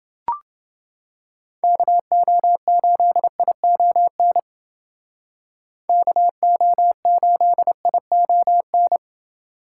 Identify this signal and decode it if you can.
Morse code, heard as a keyed tone, about 29 wpm, 700 Hz. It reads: KO8ION KO8ION